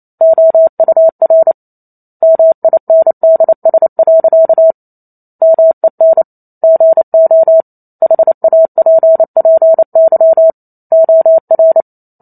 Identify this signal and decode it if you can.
Morse code, heard as a keyed tone, about 28 wpm, 650 Hz. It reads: OUR MINDS. MEN GO HAPPY OR